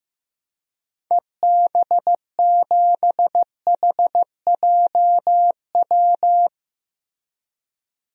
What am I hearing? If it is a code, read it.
Morse code, heard as a keyed tone, about 15 wpm, 700 Hz. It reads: EB7HJW
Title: DXCC-100 Call Signs - 6 Characters - Repeated 2x - 15wpm